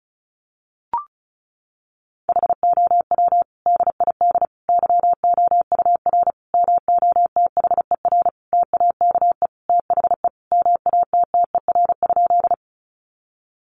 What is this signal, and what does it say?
Morse code, heard as a keyed tone, about 34 wpm, 700 Hz. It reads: HOW DID YOUR MOTHER TAKE THE MATTER?
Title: Sentences from Top 700 Words 35wpm